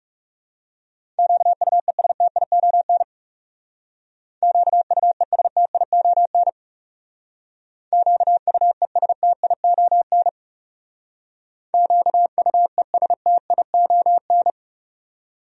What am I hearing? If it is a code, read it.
Morse code, heard as a keyed tone, about 35 wpm, 700 Hz. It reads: QUESTION QUESTION QUESTION QUESTION
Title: Speed-Racing Top 500 Words 30wpm